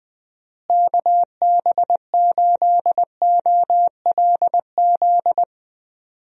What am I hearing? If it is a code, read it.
Morse code, heard as a keyed tone, about 20 wpm, 700 Hz. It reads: KB8OLZ
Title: USA Call Signs - 6 Characters - Repeated 3x - 20wpm